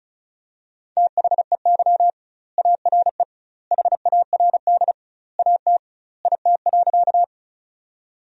Morse code, heard as a keyed tone, about 36 wpm, 700 Hz. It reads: THEY ARE HARD AT IT.
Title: 2x Word-Spacing - Sentences from Top 300 Words - 35wpm